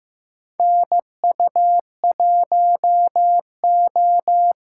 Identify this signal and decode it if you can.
Morse code, heard as a keyed tone, about 16 wpm, 700 Hz. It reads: NU1O
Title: K1USN SST QSO Practice - 15wpm